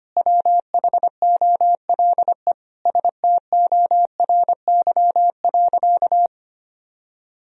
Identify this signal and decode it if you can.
Morse code, heard as a keyed tone, about 25 wpm, 700 Hz. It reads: WHOLE STORY.
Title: Sentences from Top 1000 Words - Review Part 3 - 25wpm